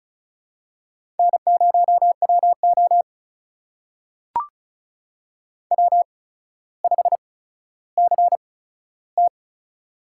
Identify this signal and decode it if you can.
Morse code, heard as a keyed tone, about 36 wpm, spaced as for 15 wpm, 700 Hz. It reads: N0WO W5CT
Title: USA Call Signs - 4 Characters - 8x Character Spacing - 35wpm